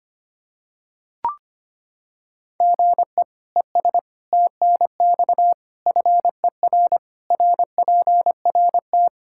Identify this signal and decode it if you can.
Morse code, heard as a keyed tone, about 25 wpm, 700 Hz. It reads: GE ES TNX FER RPRT
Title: Practice QSO - Exchange 1 - Beginning - 25wpm